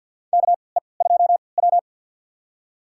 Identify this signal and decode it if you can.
Morse code, heard as a keyed tone, about 51 wpm, spaced as for 28 wpm, 700 Hz. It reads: KE2W
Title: USA Call Signs - 4 Characters - 3x Character Spacing - 50wpm